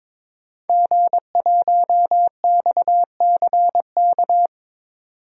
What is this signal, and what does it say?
Morse code, heard as a keyed tone, about 22 wpm, 700 Hz. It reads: G1XCK
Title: DXCC-100 Call Signs - 5 Characters - 22wpm